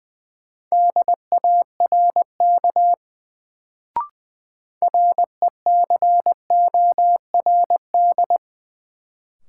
Morse code, heard as a keyed tone, about 21 wpm, 700 Hz. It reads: DARK RECORD
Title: Sentences from Top 500 Words - New Word Drill - 20wpm